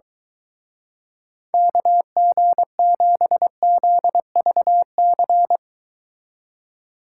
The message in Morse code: KG7ZVC

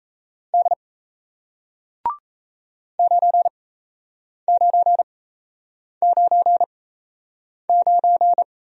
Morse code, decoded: D 9999

